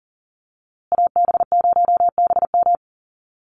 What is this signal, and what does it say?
Morse code, heard as a keyed tone, about 39 wpm, 700 Hz. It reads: AB0BM
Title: USA Call Signs Repeated 2x 40wpm